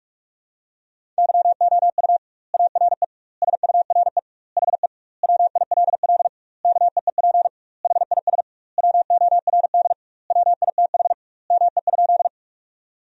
Morse code, heard as a keyed tone, about 41 wpm, 700 Hz. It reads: YOU ARE SURE HE WILL KEEP HIS WORD WITH ME?
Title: 2x Word-Spacing - Sentences from Top 300 Words - 45wpm